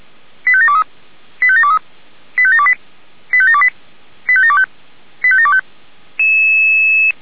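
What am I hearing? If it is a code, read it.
Five-tone selective call (ZVEI), heard as quick runs of five tones, tones of about 70 ms. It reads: 85723 85723 85728 85728 85725 85725
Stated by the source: ZVEI five-tone call 85721 received by SDR.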